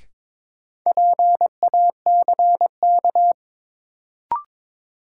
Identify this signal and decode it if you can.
Morse code, heard as a keyed tone, about 22 wpm, 700 Hz. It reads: PACK
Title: Sentences from Top 1300 Words - New Word Drill - 22wpm